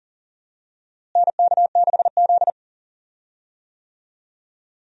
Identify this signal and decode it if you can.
Morse code, heard as a keyed tone, about 40 wpm, 700 Hz. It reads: NK6Z
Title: USA Call Signs - 4 Characters - Repeated 3x - 40wpm